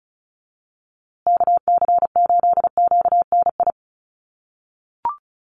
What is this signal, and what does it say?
Morse code, heard as a keyed tone, about 34 wpm, 700 Hz. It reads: KC8QNI
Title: USA Call Signs Repeated 2x 35wpm